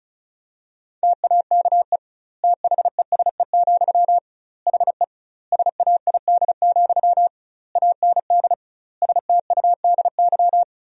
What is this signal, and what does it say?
Morse code, heard as a keyed tone, about 36 wpm, 700 Hz. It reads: TAKE THESE, HE SAID, AND STUDY
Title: 2x Word-Spacing - Sentences from Top 300 Words - 35wpm